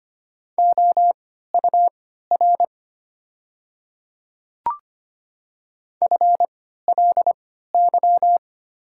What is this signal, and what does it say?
Morse code, heard as a keyed tone, about 25 wpm, spaced as for 14 wpm, 700 Hz. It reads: OUR FLY
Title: 3x Character Spacing - Top 500 Words - 25wpm